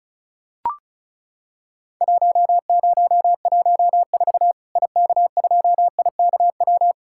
Morse code, heard as a keyed tone, about 36 wpm, 700 Hz. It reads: 1014 IK2IKW
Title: Worked All Europe DX Contest - QTC Exchange - 35wpm